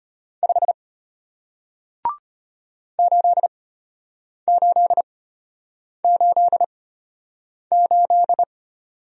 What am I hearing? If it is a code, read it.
Morse code, heard as a keyed tone, about 33 wpm, spaced as for 8 wpm, 700 Hz. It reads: 5 8888